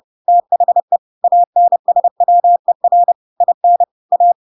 Morse code, heard as a keyed tone, about 30 wpm, 700 Hz. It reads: THE ANSWER IN A
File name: Pinocchio / Pinocchio ch8 30 30 ik1hge